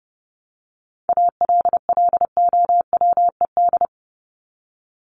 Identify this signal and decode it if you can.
Morse code, heard as a keyed tone, about 30 wpm, 700 Hz. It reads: ALLOWED
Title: ED Suffix - Top 100 Words with Base Words 30wpm